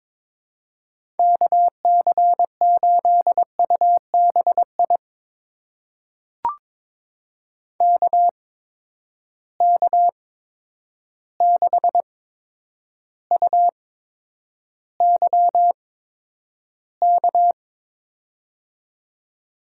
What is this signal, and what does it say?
Morse code, heard as a keyed tone, about 22 wpm, spaced as for 10 wpm, 700 Hz. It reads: KC8UBI KK6UYK
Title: USA Call Signs - 6 Characters - 8x Character Spacing - 22wpm